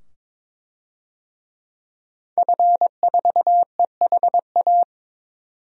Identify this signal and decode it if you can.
Morse code, heard as a keyed tone, about 22 wpm, 700 Hz. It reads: F4EHA
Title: DXCC-100 Call Signs - 5 Characters - 22wpm